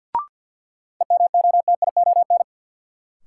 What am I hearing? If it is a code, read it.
Morse code, heard as a keyed tone, about 49 wpm, 700 Hz. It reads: EMOTION
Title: Sentences from Top 1600 Words - New Word Drill Alt - 50wpm